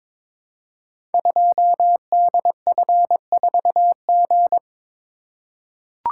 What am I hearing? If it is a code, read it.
Morse code, heard as a keyed tone, about 23 wpm, 700 Hz. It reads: 2DF4G
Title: Code Groups - 5 Letter:Numbers - 22wpm